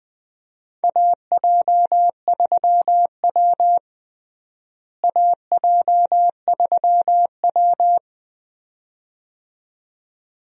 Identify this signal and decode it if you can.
Morse code, heard as a keyed tone, about 21 wpm, 700 Hz. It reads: AJ3W AJ3W
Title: USA Call Signs - 4 Characters - Repeated 2x - 20wpm